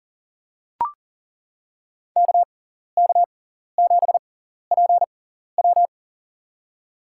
Morse code, heard as a keyed tone, about 40 wpm, spaced as for 14 wpm, 700 Hz. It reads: KK7PW
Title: USA Call Signs - 5 Characters - 6x Character Spacing - 40wpm